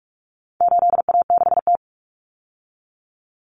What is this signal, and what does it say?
Morse code, heard as a keyed tone, about 44 wpm, 700 Hz. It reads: 8A6T
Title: Common Cut Numbers - 4-Digit 45wpm